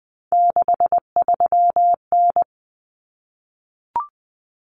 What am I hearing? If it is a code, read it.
Morse code, heard as a keyed tone, about 20 wpm, 700 Hz. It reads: 63N
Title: Common Cut Numbers - 3-Digit 20wpm